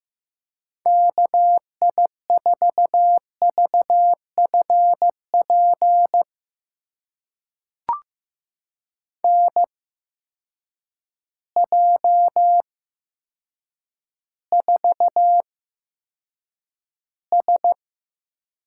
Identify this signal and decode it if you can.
Morse code, heard as a keyed tone, about 15 wpm, spaced as for 8 wpm, 700 Hz. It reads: KI4VFP NJ4S
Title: USA Call Signs - 8x Character Spacing - 15wpm